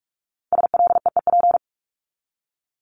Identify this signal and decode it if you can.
Morse code, heard as a keyed tone, about 44 wpm, 700 Hz. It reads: SLEEP
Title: Top 500 Words 45wpm